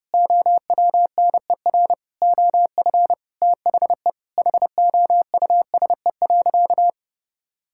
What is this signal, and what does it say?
Morse code, heard as a keyed tone, about 30 wpm, 700 Hz. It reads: OWNER OF THE HOUSE.
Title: Sentences from Top 1600 Words 30wpm